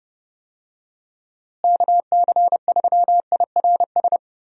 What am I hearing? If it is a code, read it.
Morse code, heard as a keyed tone, about 32 wpm, 700 Hz. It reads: KC3IRS
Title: USA Call Signs - 6 Characters - Repeated 3x - 30wpm